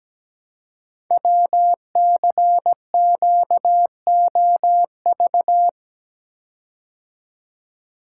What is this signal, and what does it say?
Morse code, heard as a keyed tone, about 17 wpm, 700 Hz. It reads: WCQOV
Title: Code Groups - 5 Letters - 17wpm